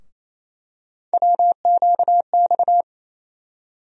Morse code, heard as a keyed tone, about 28 wpm, 700 Hz. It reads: WQX
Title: USA Call Sign Postfixes - 3 Letters 28wpm